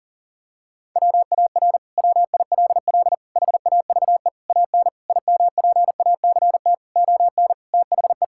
Morse code, heard as a keyed tone, about 40 wpm, 700 Hz. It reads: WAR WILL HAVE AN IMPACT ON THE
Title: Sentences from Top 1000 Words - Review Part 6 - 40wpm